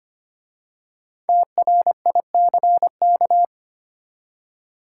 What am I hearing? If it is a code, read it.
Morse code, heard as a keyed tone, about 25 wpm, 700 Hz. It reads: TRICK
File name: 5 Letter Words - First 3 Letters Given - 25wpm